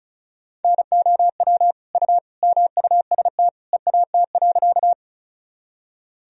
Morse code, heard as a keyed tone, about 36 wpm, 700 Hz. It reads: NOW U MUST EAT.
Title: Sentences from Top 300 Words and Abbreviations 35wpm